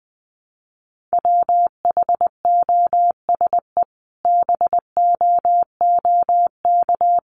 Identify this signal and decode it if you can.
Morse code, heard as a keyed tone, about 20 wpm, 700 Hz. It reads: WHOSE BOOK
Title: Sentences from Top 1200 Words 20wpm